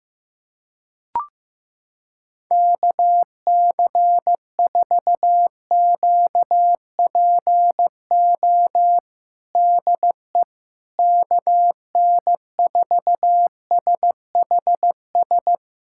Morse code, heard as a keyed tone, about 15 wpm, 700 Hz. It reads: KC4QPO DE KN4SHS
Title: USA Callsign DE Callsign 15wpm